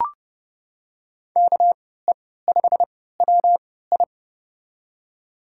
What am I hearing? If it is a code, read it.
Morse code, heard as a keyed tone, about 30 wpm, spaced as for 17 wpm, 700 Hz. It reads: KE5WI
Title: USA Call Signs - 5 Characters - 3x Character Spacing - 30wpm